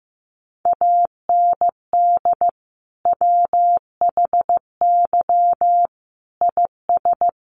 Morse code, heard as a keyed tone, about 15 wpm, 700 Hz. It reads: AND WHY IS
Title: Sentences from Top 1100 Words 15wpm